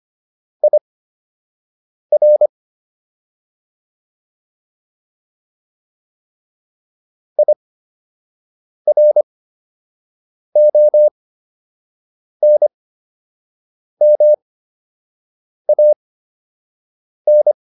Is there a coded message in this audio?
Yes: Morse code from 0.6 to 17.5 s, heard as a keyed tone, about 25 wpm, spaced as for 6 wpm, 600 Hz. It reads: IR IRONMAN